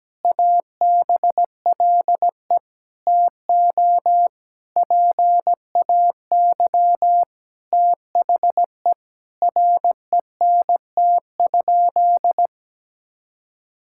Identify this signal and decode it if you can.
Morse code, heard as a keyed tone, about 17 wpm, 700 Hz. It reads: ABLE TO PAY THE RENT?
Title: Sentences from Top 1000 Words - Review Part 5 - 17wpm